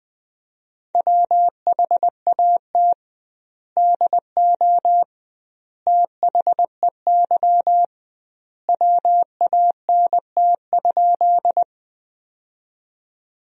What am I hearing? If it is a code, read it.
Morse code, heard as a keyed tone, about 21 wpm, 700 Hz. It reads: WHAT DO THEY WANT?
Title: 2x Word-Spacing - Sentences from Top 200 Words - 20wpm